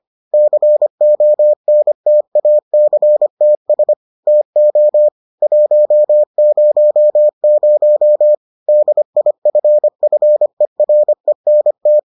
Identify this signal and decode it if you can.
Morse code, heard as a keyed tone, about 25 wpm, 600 Hz. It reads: CONTACTS TO 100 DIFFERENT